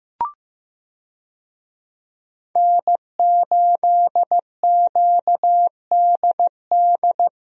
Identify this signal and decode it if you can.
Morse code, heard as a keyed tone, about 15 wpm, 700 Hz. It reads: N8QDD
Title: USA Call Signs - 5 Characters - 15wpm